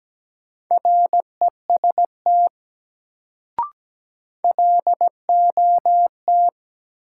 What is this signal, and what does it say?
Morse code, heard as a keyed tone, about 17 wpm, 700 Hz. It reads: REST LOT
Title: Sentences from Top 500 Words - New Word Drill - 17wpm